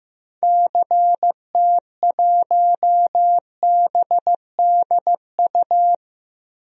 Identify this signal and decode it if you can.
Morse code, heard as a keyed tone, about 16 wpm, 700 Hz. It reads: CT1BDU
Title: DXCC-100 Call Signs - 6 Characters - Repeated 2x - 15wpm